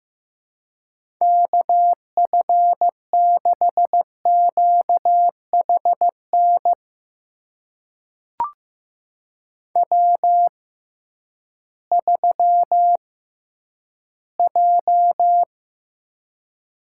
Morse code, heard as a keyed tone, about 15 wpm, spaced as for 7 wpm, 700 Hz. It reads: KF6QH NW3J
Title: USA Call Signs - 6x Character Spacing - 15wpm